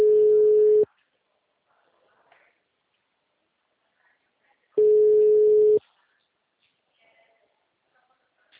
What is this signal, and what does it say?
Ringback tone (Europe, 425 Hz).